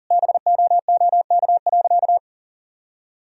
Morse code, heard as a keyed tone, about 37 wpm, 700 Hz. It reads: BOOK.
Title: Sentences from Top 1000 Words - Review Part 2 - 40wpm